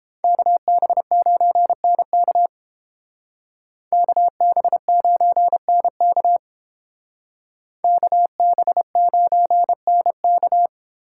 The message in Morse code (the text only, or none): KB9NK KB9NK KB9NK